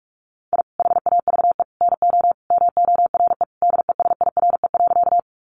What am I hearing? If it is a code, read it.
Morse code, heard as a keyed tone, about 44 wpm, 700 Hz. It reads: I HAVE NO MORE DESIRE.
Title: Sentences from Top 1100 Words 45wpm